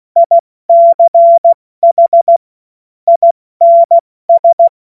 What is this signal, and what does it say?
Morse code, heard as a keyed tone, about 16 wpm, 680 Hz. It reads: ICH INS
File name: Morsekurs / Witze / Witz 21. September 2021, 70 F80